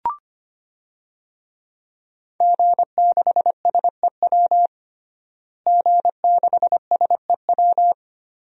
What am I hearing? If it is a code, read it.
Morse code, heard as a keyed tone, about 25 wpm, 700 Hz. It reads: G6SEW G6SEW
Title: DXCC-100 Call Signs - 5 Characters - Repeated 2x - 25wpm